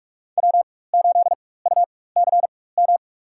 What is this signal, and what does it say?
Morse code, heard as a keyed tone, about 48 wpm, spaced as for 21 wpm, 700 Hz. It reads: W8UCM
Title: USA Call Signs - 5 Characters - 4x Character Spacing - 45wpm